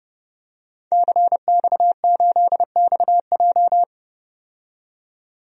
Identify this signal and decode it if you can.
Morse code, heard as a keyed tone, about 30 wpm, 700 Hz. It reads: CX8XJ
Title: DXCC-100 Call Signs - 5 Characters - 30wpm